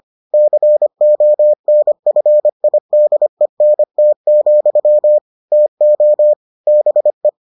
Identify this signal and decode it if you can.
Morse code, heard as a keyed tone, about 25 wpm, 600 Hz. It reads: CONFIDENT, TO BE